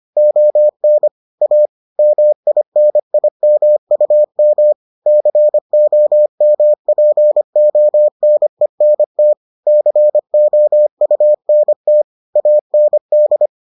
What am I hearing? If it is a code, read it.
Morse code, heard as a keyed tone, about 25 wpm, 600 Hz. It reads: ON A MINIMUM COMPONENT COUNT AND